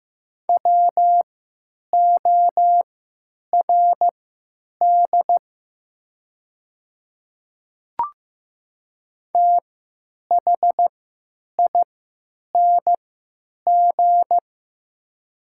Morse code, heard as a keyed tone, about 15 wpm, spaced as for 9 wpm, 700 Hz. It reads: WORD THING
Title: 3x Character Spacing - Top 100 Words - 15wpm